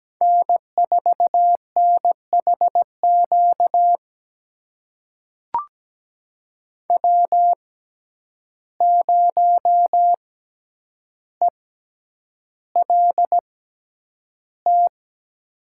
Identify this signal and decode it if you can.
Morse code, heard as a keyed tone, about 17 wpm, spaced as for 9 wpm, 700 Hz. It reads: N4NHQ W0ELT
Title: USA Call Signs - 5 Characters - 6x Character Spacing - 17wpm